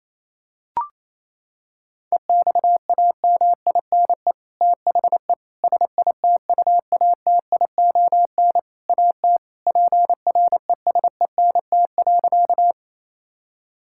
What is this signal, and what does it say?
Morse code, heard as a keyed tone, about 28 wpm, 700 Hz. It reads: EXAMINE THE SITUATION AT PRESENT.
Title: Sentences from Top 1600 Words 28wpm